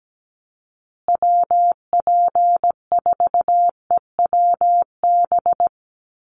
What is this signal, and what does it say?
Morse code, heard as a keyed tone, about 17 wpm, 700 Hz. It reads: WP4EWB
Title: USA Call Signs Repeated 2x 17wpm